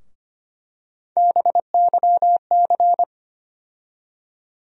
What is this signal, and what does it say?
Morse code, heard as a keyed tone, about 25 wpm, 700 Hz. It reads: BYC
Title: USA Call Sign Postfixes - 3 Letters 25wpm